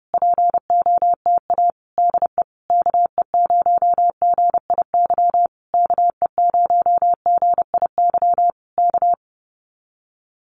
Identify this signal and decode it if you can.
Morse code, heard as a keyed tone, about 30 wpm, 700 Hz. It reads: POTA DE KE0GIY KE0GIY K
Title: Common QSO Phrases Encore 30wpm